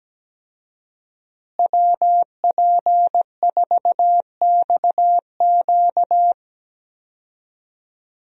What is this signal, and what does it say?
Morse code, heard as a keyed tone, about 17 wpm, 700 Hz. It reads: WP4XQ
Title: USA Call Signs - 5 Characters - Repeated 2x - 17wpm